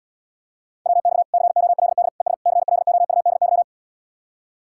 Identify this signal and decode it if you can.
Morse code, heard as a keyed tone, about 79 wpm, 700 Hz. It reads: WX COLD ES CLOUDY